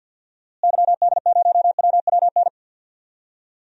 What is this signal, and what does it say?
Morse code, heard as a keyed tone, about 51 wpm, 700 Hz. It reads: YD0WWN
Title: DXCC-100 Call Signs - 6 Characters - Repeated 3x - 50wpm